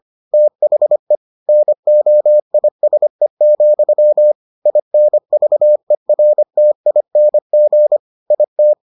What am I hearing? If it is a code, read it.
Morse code, heard as a keyed tone, about 25 wpm, 600 Hz. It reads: THE NOISE, INVERTING IT